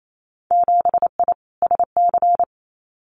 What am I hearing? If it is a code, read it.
Morse code, heard as a keyed tone, about 28 wpm, 690 Hz. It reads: 7I SC
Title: Winter Field Day Exchanges Encore 28wpm